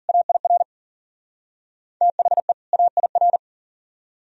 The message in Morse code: AIR THE AIR

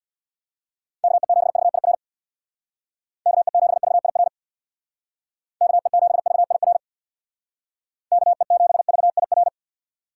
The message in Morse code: KE7VIR KE7VIR KE7VIR KE7VIR

